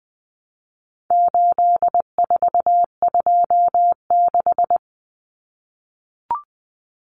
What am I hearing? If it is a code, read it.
Morse code, heard as a keyed tone, about 20 wpm, 700 Hz. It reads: 8426
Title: Numbers - 4-Digit 20wpm